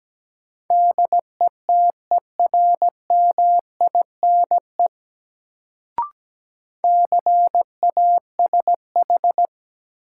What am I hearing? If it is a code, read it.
Morse code, heard as a keyed tone, about 17 wpm, 700 Hz. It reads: DETERMINE CASH
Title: Sentences from Top 700 Words - New Word Drill - 17wpm